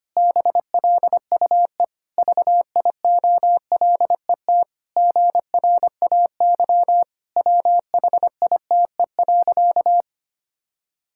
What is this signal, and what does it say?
Morse code, heard as a keyed tone, about 25 wpm, 700 Hz. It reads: BLUE VIOLET GRAY WHITE.
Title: Pangram Challenge v2 - 25wpm